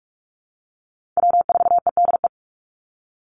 Morse code, heard as a keyed tone, about 44 wpm, 700 Hz. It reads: W4EDE
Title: Call Signs - 45wpm